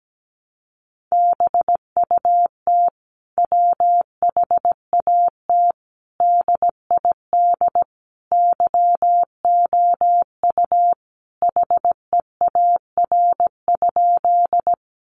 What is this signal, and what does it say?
Morse code, heard as a keyed tone, about 17 wpm, 700 Hz. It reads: BUT WHAT DID YOU HEAR?
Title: Sentences from Top 300 Words 17wpm